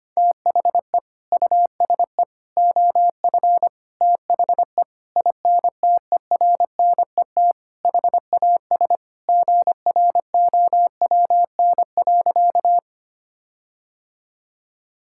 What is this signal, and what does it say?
Morse code, heard as a keyed tone, about 25 wpm, 700 Hz. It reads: THE USE OF THE INTERNET HAS GROWN.
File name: practice / 25wpm / Sentences with 4-Word Phrases - 25wpm